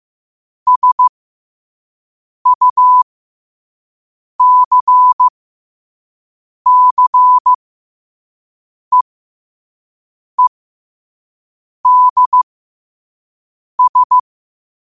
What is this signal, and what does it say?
Morse code, heard as a keyed tone, about 15 wpm, spaced as for 5 wpm, 1000 Hz. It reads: SUCCEEDS